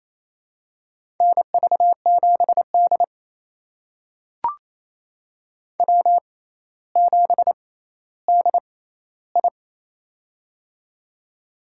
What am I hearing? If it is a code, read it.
Morse code, heard as a keyed tone, about 28 wpm, spaced as for 14 wpm, 700 Hz. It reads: NV7D W7DI
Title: USA Call Signs - 4 Characters - 6x Character Spacing - 28wpm